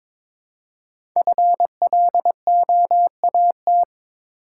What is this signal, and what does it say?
Morse code, heard as a keyed tone, about 23 wpm, 700 Hz. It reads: FLOAT
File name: 5 Letter Words - First 3 Letters Given - 22wpm